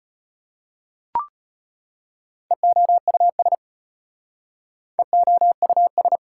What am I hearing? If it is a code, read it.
Morse code, heard as a keyed tone, about 34 wpm, 700 Hz. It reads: EOUS EOUS